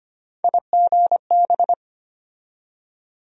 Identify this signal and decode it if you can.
Morse code, heard as a keyed tone, about 25 wpm, 700 Hz. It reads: IGB
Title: USA Call Sign Postfixes - 3 Letters 25wpm